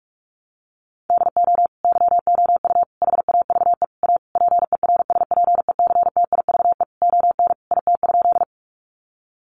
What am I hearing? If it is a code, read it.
Morse code, heard as a keyed tone, about 44 wpm, 700 Hz. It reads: DO YOU HAVE A PERSPECTIVE ON IT?
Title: Sentences from Top 1200 Words 45wpm